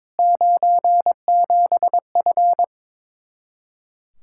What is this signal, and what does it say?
Morse code, heard as a keyed tone, about 22 wpm, 700 Hz. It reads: 97F
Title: Speed-Racing Common QSO Phrases Encore - 22wpm